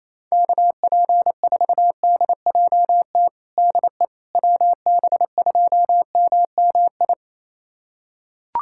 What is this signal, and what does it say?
Morse code, heard as a keyed tone, about 28 wpm, 700 Hz. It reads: KP4DJT DE WB2MMI
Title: USA Callsign DE Callsign - 6 Characters - 28wpm